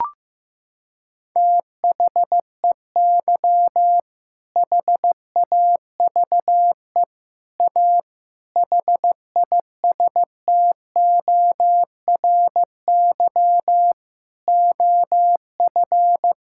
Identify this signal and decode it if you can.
Morse code, heard as a keyed tone, about 15 wpm, 700 Hz. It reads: THEY HAVE A HISTORY OF
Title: Sentences from Top 1000 Words - Review Part 5 - 15wpm